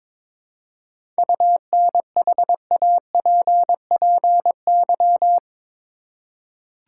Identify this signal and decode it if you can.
Morse code, heard as a keyed tone, about 22 wpm, 700 Hz. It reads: UNHAPPY
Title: Seven Letter Words - 22wpm